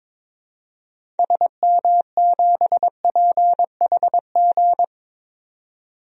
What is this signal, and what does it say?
Morse code, heard as a keyed tone, about 22 wpm, 700 Hz. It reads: SM7PHG
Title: DXCC-100 Call Signs - 6 Characters - 22wpm